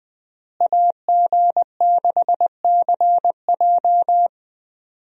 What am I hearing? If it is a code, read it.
Morse code, heard as a keyed tone, about 20 wpm, 700 Hz. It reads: AG6CJ
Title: USA Call Signs - 5 Characters - Repeated 3x - 20wpm